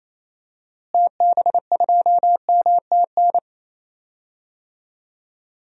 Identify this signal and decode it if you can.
Morse code, heard as a keyed tone, about 28 wpm, 700 Hz. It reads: TB2MTN
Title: DXCC-100 Call Signs - 6 Characters - Repeated 2x - 28wpm